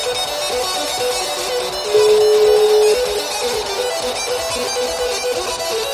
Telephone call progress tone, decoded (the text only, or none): ringback tone